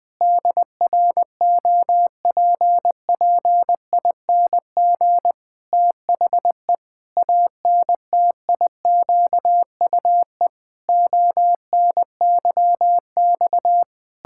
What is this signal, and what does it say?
Morse code, heard as a keyed tone, about 20 wpm, 700 Hz. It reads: DROPPING THE ANTIQUE ONYX
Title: Pangram Challenge v2 - 20wpm